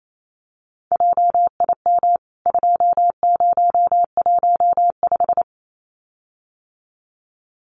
Morse code, heard as a keyed tone, about 28 wpm, 700 Hz. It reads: JIM 2015
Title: CWT Exchange 28wpm